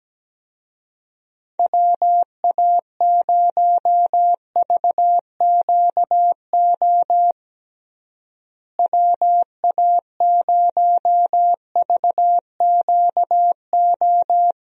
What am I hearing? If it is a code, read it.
Morse code, heard as a keyed tone, about 17 wpm, 700 Hz. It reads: WA0VQO WA0VQO
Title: USA Call Signs - 6 Characters - Repeated 3x - 17wpm